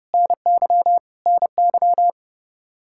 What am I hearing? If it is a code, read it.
Morse code, heard as a keyed tone, about 32 wpm, 700 Hz. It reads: NY NY
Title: Practice QSO - Exchange 1 - Beginning - 30wpm